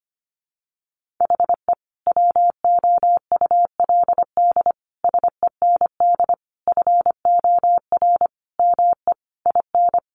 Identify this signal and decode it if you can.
Morse code, heard as a keyed tone, about 25 wpm, 700 Hz. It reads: HE WOULD SEND FOR ME IN
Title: Sentences from Top 1500 Words 25wpm